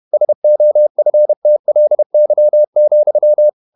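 Morse code, heard as a keyed tone, about 31 wpm, 600 Hz. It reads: SOFTLY,
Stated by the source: Morse code at 30 wpm, the novel Never by Ken Follett in English, from CW med Gnister.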